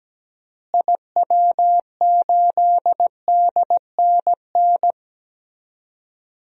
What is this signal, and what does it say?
Morse code, heard as a keyed tone, about 17 wpm, 700 Hz. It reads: IW8DNN
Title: DXCC-100 Call Signs - 6 Characters - Repeated 3x - 17wpm